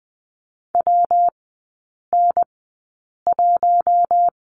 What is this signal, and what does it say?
Morse code, heard as a keyed tone, about 20 wpm, spaced as for 8 wpm, 690 Hz. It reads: WN1
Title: 166 Lesson 27e - Number 7 - Three Characters 20wpm